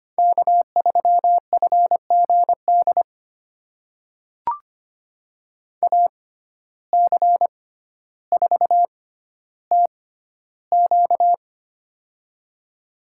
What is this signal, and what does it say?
Morse code, heard as a keyed tone, about 25 wpm, spaced as for 13 wpm, 700 Hz. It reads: K3FGD AC4TQ
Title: USA Call Signs - 5 Characters - 6x Character Spacing - 25wpm